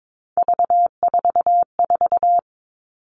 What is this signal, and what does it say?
Morse code, heard as a keyed tone, about 22 wpm, 700 Hz. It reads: V44
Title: Three Challenging Letter-Number Combinations 22wpm